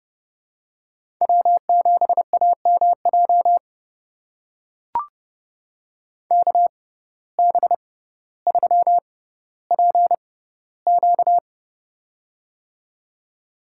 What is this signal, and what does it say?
Morse code, heard as a keyed tone, about 31 wpm, spaced as for 15 wpm, 700 Hz. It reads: W7AMJ KB3PQ
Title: USA Call Signs - 5 Characters - 6x Character Spacing - 30wpm